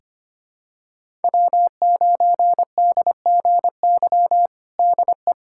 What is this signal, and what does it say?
Morse code, heard as a keyed tone, about 25 wpm, 700 Hz. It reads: W9DGY DE
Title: USA Callsign DE Callsign - 5 Characters - 25wpm